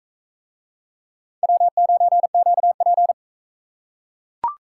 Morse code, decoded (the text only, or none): W9QP